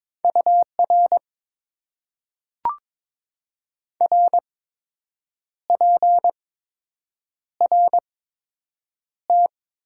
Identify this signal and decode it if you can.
Morse code, heard as a keyed tone, about 22 wpm, spaced as for 7 wpm, 700 Hz. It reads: UR RPRT
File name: Top 90 QSO Elements - 8x Character Spacing - 22wpm